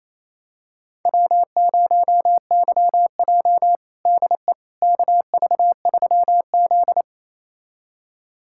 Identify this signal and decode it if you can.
Morse code, heard as a keyed tone, about 28 wpm, 700 Hz. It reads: W0YJ DE KV3Z
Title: USA Callsign DE Callsign - 4 Characters - 28wpm